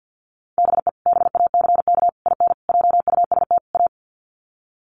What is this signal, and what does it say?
Morse code, heard as a keyed tone, about 48 wpm, 700 Hz. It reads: BE BACK IN JUST A
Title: Sets of 5 Words Encore 50wpm